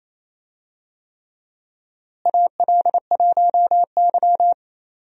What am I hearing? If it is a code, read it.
Morse code, heard as a keyed tone, about 28 wpm, 700 Hz. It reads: AL1Y
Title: USA Call Signs - 4 Characters - Repeated 3x - 28wpm